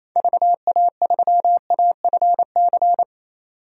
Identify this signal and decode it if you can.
Morse code, heard as a keyed tone, about 27 wpm, 700 Hz. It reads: VA3AFC
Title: DXCC-100 Call Signs - 6 Characters - 28wpm